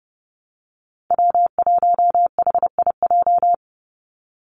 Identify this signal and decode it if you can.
Morse code, heard as a keyed tone, about 30 wpm, 700 Hz. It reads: W1HIJ
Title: Call Signs Repeated 3x - 30wpm